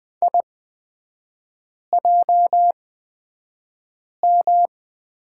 Morse code, heard as a keyed tone, about 20 wpm, spaced as for 5 wpm, 700 Hz. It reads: IJM
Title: 236 Lesson 38e - Letter q - Three Characters 20wpm